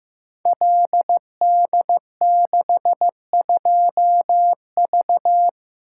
Morse code, heard as a keyed tone, about 15 wpm, 700 Hz. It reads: LD62V
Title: Code Groups - 5 Letter:Numbers - 15wpm